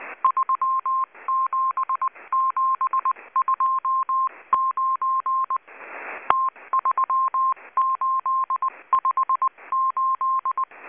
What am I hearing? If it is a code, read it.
Morse code, heard as a keyed tone, about 19 wpm, 1060 Hz. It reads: 37729 T3858
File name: HM02- -0540utc